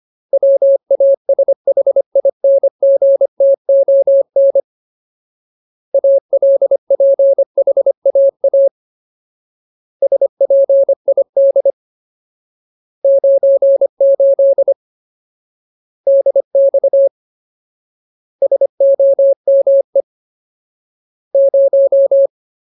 Morse code, heard as a keyed tone, about 25 wpm, 550 Hz. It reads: WASHINGTON ALPHAA SPID 98 DX SOME 0
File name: Ultimate Word List / 25WPM Ultimate